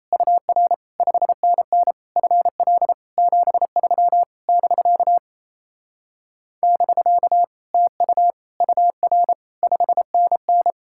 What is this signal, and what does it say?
Morse code, heard as a keyed tone, about 37 wpm, 700 Hz. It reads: UR 5NN FL 73 <BK> <BK> TU UR 5NN